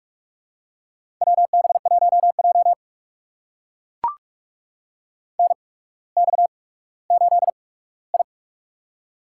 Morse code, heard as a keyed tone, about 44 wpm, spaced as for 19 wpm, 700 Hz. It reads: WB1J NX8I
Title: USA Call Signs - 4 Characters - 8x Character Spacing - 45wpm